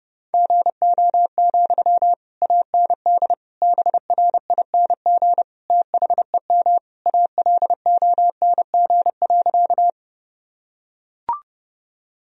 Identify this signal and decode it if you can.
Morse code, heard as a keyed tone, about 29 wpm, 700 Hz. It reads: GO, AND BRING THEM ALONG.
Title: Sentences from Top 1000 Words - Review Part 3 - 30wpm